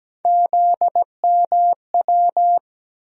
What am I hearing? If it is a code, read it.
Morse code, heard as a keyed tone, about 17 wpm, 700 Hz. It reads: ZMW
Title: USA Call Sign Postfixes - 3 Letters 17wpm